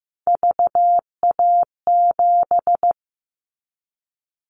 Morse code, heard as a keyed tone, about 15 wpm, 700 Hz. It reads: VA7
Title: Callsign Prefixes for Top 50 DXCC - 15wpm